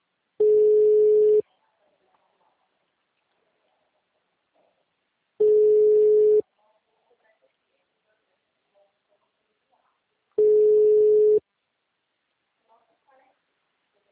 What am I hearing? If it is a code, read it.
Ringback tone (Europe, 425 Hz), with speech.
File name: out- -2021- - - .19287